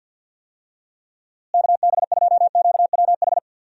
Morse code, heard as a keyed tone, about 53 wpm, 700 Hz. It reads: KB2QWH
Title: USA Call Signs - 6 Characters - Repeated 3x - 50wpm